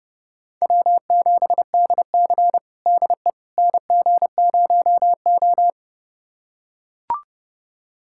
Morse code, heard as a keyed tone, about 32 wpm, 700 Hz. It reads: W7DC DE NG0O
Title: USA Callsign DE Callsign - 4 Characters - 30wpm